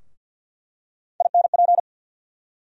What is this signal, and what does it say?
Morse code, heard as a keyed tone, about 48 wpm, 700 Hz. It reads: INP
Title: USA Call Sign Postfixes - 3 Letters 50wpm